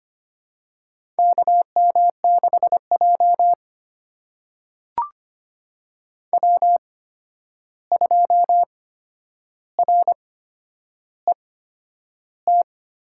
Morse code, heard as a keyed tone, about 25 wpm, spaced as for 10 wpm, 700 Hz. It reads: KM6J W2RET
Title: USA Call Signs - 8x Character Spacing - 25wpm